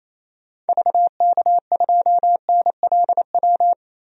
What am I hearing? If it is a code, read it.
Morse code, heard as a keyed tone, about 28 wpm, 700 Hz. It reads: VK2NLW
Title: DXCC-100 Call Signs - 6 Characters - 28wpm